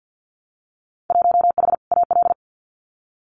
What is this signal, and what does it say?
Morse code, heard as a keyed tone, about 50 wpm, 700 Hz. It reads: 1H AL
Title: Winter Field Day Exchanges 50wpm